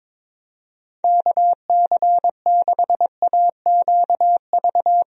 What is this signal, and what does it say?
Morse code, heard as a keyed tone, about 22 wpm, 700 Hz. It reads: KC6AQV
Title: USA Call Signs - 6 Characters - Repeated 2x - 22wpm